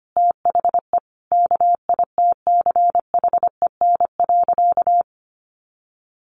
Morse code, heard as a keyed tone, about 25 wpm, 700 Hz. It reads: THE KITCHEN.
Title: Sentences from Top 1400 Words 25wpm